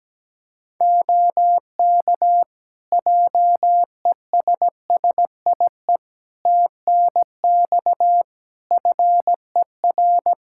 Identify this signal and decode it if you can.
Morse code, heard as a keyed tone, about 17 wpm, 700 Hz. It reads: OK JESSIE TNX FER
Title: Practice QSO - Exchange 4 - Ending - 17wpm